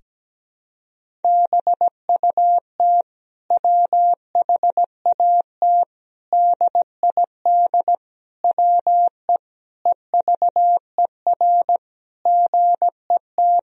Morse code, heard as a keyed tone, about 17 wpm, 700 Hz. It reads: BUT WHAT DID WE EVER GET
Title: Sentences from Top 400 Words and Abbreviations 17wpm